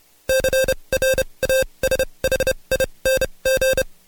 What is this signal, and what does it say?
Morse code, heard as a keyed tone, about 29 wpm, spaced as for 23 wpm, 530 Hz. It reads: CRASHING
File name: Code Quick / AudioDownld2 / Seg138-Stepper11 23WPM